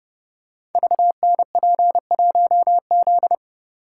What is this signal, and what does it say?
Morse code, heard as a keyed tone, about 28 wpm, 700 Hz. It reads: VNP1Z
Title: Code Groups - 5 Letter:Numbers - 30wpm